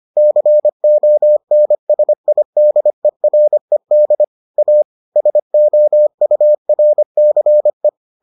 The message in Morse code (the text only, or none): CONSIDERED A SOURCE